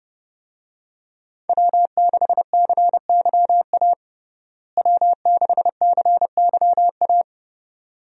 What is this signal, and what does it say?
Morse code, heard as a keyed tone, about 31 wpm, 700 Hz. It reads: W6CYA W6CYA
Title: USA Call Signs - 5 Characters - Repeated 2x - 30wpm